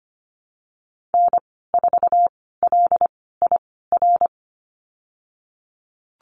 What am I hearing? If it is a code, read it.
Morse code, heard as a keyed tone, about 25 wpm, spaced as for 16 wpm, 700 Hz. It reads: N4LIR
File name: Lesson 15g - Number 5 - Call Signs 25wpm